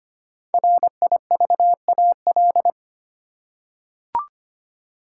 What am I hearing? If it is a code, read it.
Morse code, heard as a keyed tone, about 25 wpm, 700 Hz. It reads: RIVAL
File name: 5 Letter Words - First 3 Letters Given - 25wpm